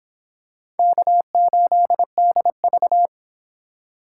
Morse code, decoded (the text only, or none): K8DV